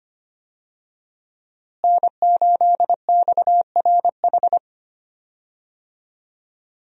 Morse code, heard as a keyed tone, about 25 wpm, 700 Hz. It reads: N8XRH
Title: USA Call Signs - 5 Characters - Repeated 3x - 25wpm